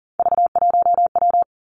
Morse code, heard as a keyed tone, about 40 wpm, 700 Hz. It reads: V1W
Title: Three Challenging Letter-Number Combinations 40wpm